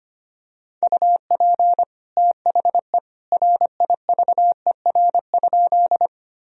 Morse code, heard as a keyed tone, about 25 wpm, 700 Hz. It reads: UP THE RIVER?
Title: Sentences from Top 1000 Words - Review Part 1 - 25wpm